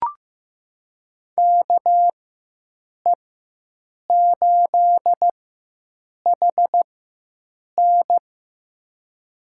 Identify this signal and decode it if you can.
Morse code, heard as a keyed tone, about 15 wpm, spaced as for 7 wpm, 700 Hz. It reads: KE8HN
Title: USA Call Signs - 5 Characters - 4x Character Spacing - 15wpm